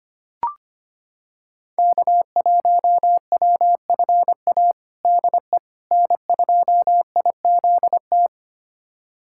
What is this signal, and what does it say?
Morse code, heard as a keyed tone, about 25 wpm, 700 Hz. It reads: K1WFA DE N2IZT
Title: USA Callsign DE Callsign - 5 Characters - 25wpm